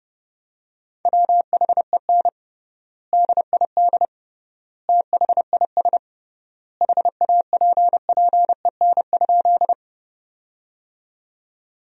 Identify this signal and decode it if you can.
Morse code, heard as a keyed tone, about 30 wpm, 700 Hz. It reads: WHEN DID THIS HAPPEN?
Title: 3x Word-Spacing - Sentences from Top 400 Words - 30wpm